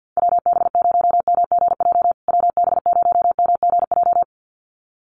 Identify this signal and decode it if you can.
Morse code, heard as a keyed tone, about 50 wpm, 700 Hz. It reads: WB0MGJ WB0MGJ
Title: USA Call Signs Repeated 2x 50wpm